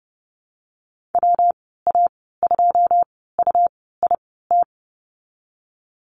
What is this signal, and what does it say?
Morse code, heard as a keyed tone, about 32 wpm, 700 Hz. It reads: W A 2 U I T
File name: Lesson 19g - Letter w - Call Signs 30wpm